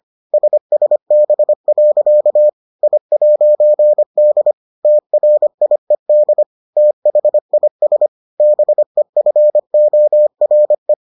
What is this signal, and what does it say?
Morse code, heard as a keyed tone, about 25 wpm, 600 Hz. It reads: SSB. I'D TRIED THIS BEFORE